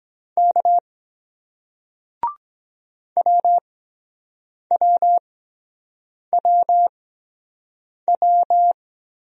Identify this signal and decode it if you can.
Morse code, heard as a keyed tone, about 23 wpm, 700 Hz. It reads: K W W W W